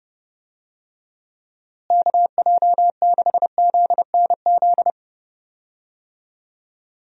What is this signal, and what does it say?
Morse code, heard as a keyed tone, about 30 wpm, 700 Hz. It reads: KJ6ZNZ
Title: USA Call Signs - 6 Characters - Repeated 3x - 30wpm